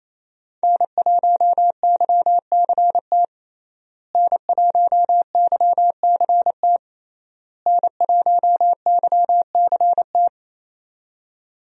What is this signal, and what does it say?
Morse code, heard as a keyed tone, about 28 wpm, 700 Hz. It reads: N1YCT N1YCT N1YCT
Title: USA Call Signs - 5 Characters - Repeated 3x - 28wpm